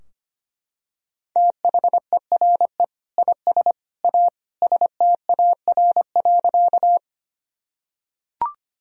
Morse code, heard as a keyed tone, about 25 wpm, 700 Hz. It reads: THERE IS A STAR.
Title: Sentences from Top 1000 Words - Review Part 2 - 25wpm